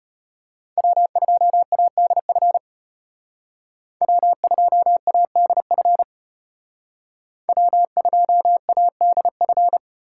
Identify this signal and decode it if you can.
Morse code, heard as a keyed tone, about 32 wpm, 700 Hz. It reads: W2ADF W2ADF W2ADF